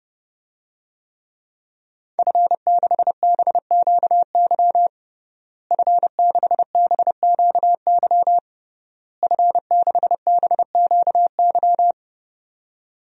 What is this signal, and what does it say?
Morse code, heard as a keyed tone, about 32 wpm, 700 Hz. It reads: F6BQY F6BQY F6BQY
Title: DXCC-100 Call Signs - 5 Characters - Repeated 3x - 30wpm